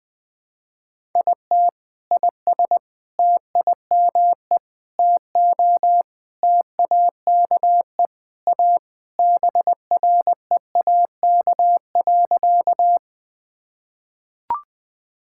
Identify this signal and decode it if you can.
Morse code, heard as a keyed tone, about 20 wpm, 700 Hz. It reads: IT IS TIME TO TAKE A BREAK.
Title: Sentences with 4-Word Phrases - 20wpm